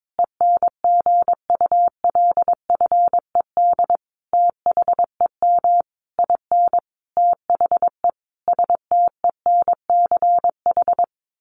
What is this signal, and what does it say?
Morse code, heard as a keyed tone, about 22 wpm, 700 Hz. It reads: ENGULFED THEM IN THE STENCH
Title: A Sound Of Thunder 22wpm